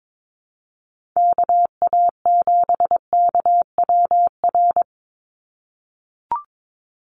Morse code, heard as a keyed tone, about 22 wpm, 700 Hz. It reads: KA7KWR
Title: USA Call Signs 22wpm